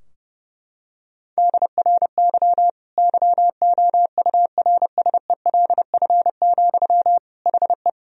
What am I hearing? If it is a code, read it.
Morse code, heard as a keyed tone, about 32 wpm, 700 Hz. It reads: DRY YOURSELF, HE
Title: Sentences from Top 1000 Words - Review Part 3 - 30wpm